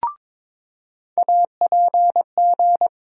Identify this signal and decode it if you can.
Morse code, heard as a keyed tone, about 22 wpm, 700 Hz. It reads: APG
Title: USA Call Sign Postfixes - 3 Letters 22wpm